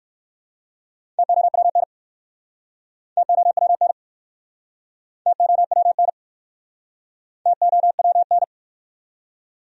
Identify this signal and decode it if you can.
Morse code, heard as a keyed tone, about 65 wpm, 700 Hz. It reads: TOWN TOWN TOWN TOWN